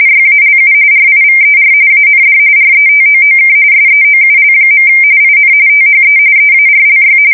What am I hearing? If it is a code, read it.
RTTY, about 45.45 baud. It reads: YALU RIVER AND RETOOK SEOUL. IN THE EARLY
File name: fsk170 / USS-New-Jersey